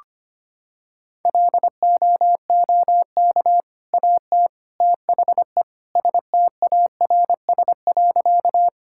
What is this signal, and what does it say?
Morse code, heard as a keyed tone, about 25 wpm, 700 Hz. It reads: LOOK AT THE STARS.
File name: practice / 25wpm / Sentences with 3-Word Phrases - 25wpm